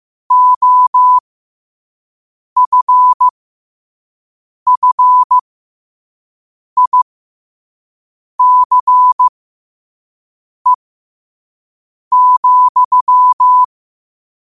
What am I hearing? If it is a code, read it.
Morse code, heard as a keyed tone, about 15 wpm, spaced as for 5 wpm, 1000 Hz. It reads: OFFICE,
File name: qotdfiles / 5wpm